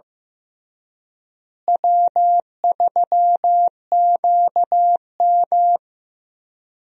Morse code, heard as a keyed tone, about 15 wpm, 700 Hz. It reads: W3QM
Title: USA Call Signs - 4 Characters - Repeated 3x - 15wpm